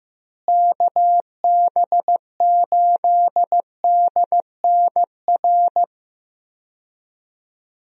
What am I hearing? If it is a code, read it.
Morse code, heard as a keyed tone, about 15 wpm, 700 Hz. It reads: KB8DNR
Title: USA Call Signs - 6 Characters - Repeated 2x - 15wpm